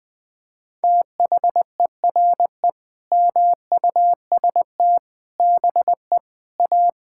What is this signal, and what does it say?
Morse code, heard as a keyed tone, about 20 wpm, 700 Hz. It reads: THERE MUST BE A
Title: Sentences from Top 1000 Words - Review Part 6 - 20wpm